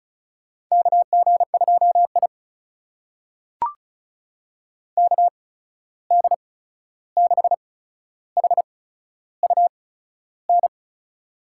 Morse code, heard as a keyed tone, about 36 wpm, spaced as for 13 wpm, 700 Hz. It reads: KG2I KD6HUN
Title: USA Call Signs - 8x Character Spacing - 35wpm